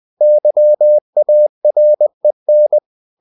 Morse code, heard as a keyed tone, about 20 wpm, 600 Hz. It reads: YAREN
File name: CW / Practice / NationalCapitals20